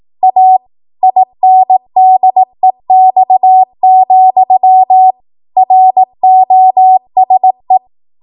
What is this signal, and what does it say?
Morse code, heard as a keyed tone, about 18 wpm, 750 Hz. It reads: A INDEX, ROSE
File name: Archive / 18 WPM / 18WPM